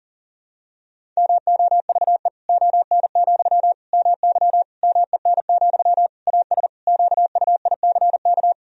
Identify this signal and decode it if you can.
Morse code, heard as a keyed tone, about 40 wpm, 700 Hz. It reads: MOVE ON, MY MEN, AS QUICK
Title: Sentences from Top 500 Words and Abbreviations 40wpm